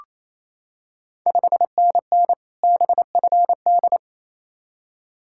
Morse code, heard as a keyed tone, about 27 wpm, 690 Hz. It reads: 5NN BFD
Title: Ontario QSO Party Exchange 28wpm